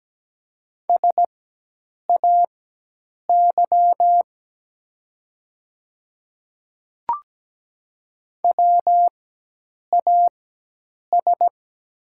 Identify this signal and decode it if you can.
Morse code, heard as a keyed tone, about 17 wpm, spaced as for 8 wpm, 700 Hz. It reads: SAY WAS
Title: 4x Character Spacing - Top 200 Words - 17wpm